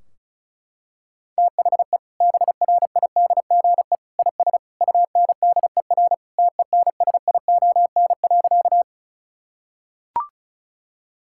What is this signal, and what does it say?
Morse code, heard as a keyed tone, about 36 wpm, 700 Hz. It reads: THE BRIDGE IS UNDER TENSION.
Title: Sentences from Top 1600 Words 35wpm